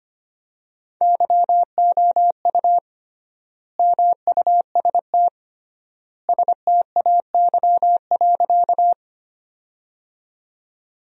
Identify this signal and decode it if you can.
Morse code, heard as a keyed tone, about 25 wpm, 700 Hz. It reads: YOU MUST STAY.
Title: 3x Word-Spacing - Sentences from Top 500 Words - 25wpm